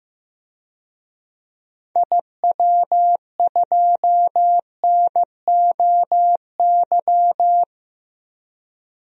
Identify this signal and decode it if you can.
Morse code, heard as a keyed tone, about 16 wpm, 700 Hz. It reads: IW2NOY
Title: DXCC-100 Call Signs - 6 Characters - Repeated 2x - 15wpm